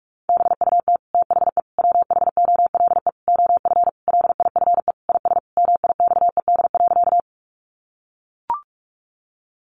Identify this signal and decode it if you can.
Morse code, heard as a keyed tone, about 44 wpm, 700 Hz. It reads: BUT THE WHOLE OF LIFE IS MIXED.
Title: Sentences from Top 1500 Words 45wpm